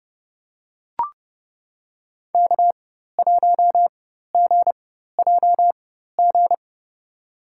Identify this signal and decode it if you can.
Morse code, heard as a keyed tone, about 30 wpm, spaced as for 14 wpm, 700 Hz. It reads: K1GJG
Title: USA Call Signs - 5 Characters - 4x Character Spacing - 30wpm